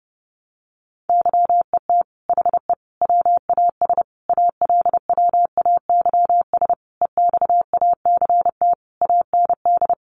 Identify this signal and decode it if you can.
Morse code, heard as a keyed tone, about 28 wpm, 700 Hz. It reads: YET HE WAS ALWAYS EXACT AND
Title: Sentences from Top 1300 Words 30wpm